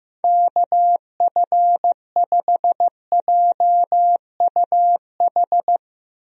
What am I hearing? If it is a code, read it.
Morse code, heard as a keyed tone, about 15 wpm, 700 Hz. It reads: KF5JUH
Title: USA Call Signs - 6 Characters - Repeated 2x - 15wpm